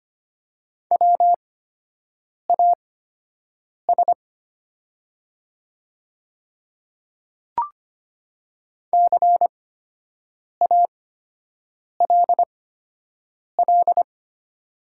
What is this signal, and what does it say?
Morse code, heard as a keyed tone, about 25 wpm, spaced as for 7 wpm, 700 Hz. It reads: WAS CALL
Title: 8x Character Spacing - Top 200 Words - 25wpm